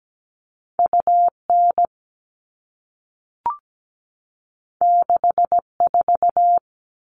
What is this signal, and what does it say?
Morse code, heard as a keyed tone, about 17 wpm, 700 Hz. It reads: UN 64
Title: All Cut Numbers - 2-Digit 17wpm